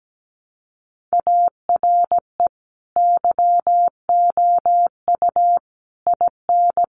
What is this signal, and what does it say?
Morse code, heard as a keyed tone, about 17 wpm, 700 Hz. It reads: ARE YOU IN
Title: Sentences from Top 1100 Words 17wpm